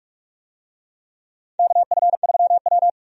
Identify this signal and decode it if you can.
Morse code, heard as a keyed tone, about 43 wpm, 700 Hz. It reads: KF3W
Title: USA Call Signs - 4 Characters - 45wpm